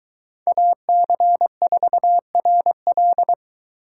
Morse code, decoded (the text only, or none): AC4RL